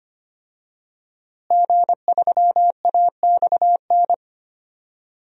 Morse code, heard as a keyed tone, about 25 wpm, 700 Hz. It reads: G3AXN
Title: DXCC-100 Call Signs - 5 Characters - Repeated 3x - 25wpm